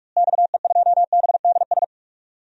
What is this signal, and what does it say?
Morse code, heard as a keyed tone, about 43 wpm, 700 Hz. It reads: XE2BDS
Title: DXCC-100 Call Signs - 6 Characters - Repeated 3x - 45wpm